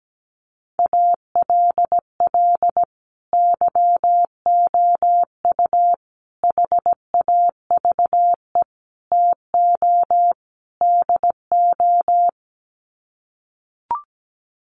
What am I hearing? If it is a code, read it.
Morse code, heard as a keyed tone, about 17 wpm, 700 Hz. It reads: ALL YOU HAVE TO DO
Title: Sets of 5 Words 17wpm